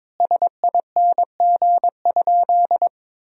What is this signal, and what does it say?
Morse code, heard as a keyed tone, about 21 wpm, 700 Hz. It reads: SING?
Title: 2x Word-Spacing - Sentences from Top 400 Words - 22wpm